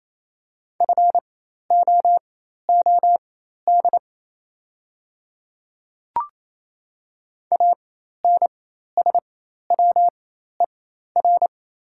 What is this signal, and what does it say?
Morse code, heard as a keyed tone, about 28 wpm, spaced as for 13 wpm, 700 Hz. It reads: FOOD ANSWER
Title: 4x Character Spacing - Top 300 Words - 28wpm